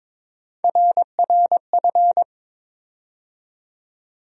Morse code, heard as a keyed tone, about 21 wpm, 700 Hz. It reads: RRF
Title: USA Call Sign Postfixes - 3 Letters 22wpm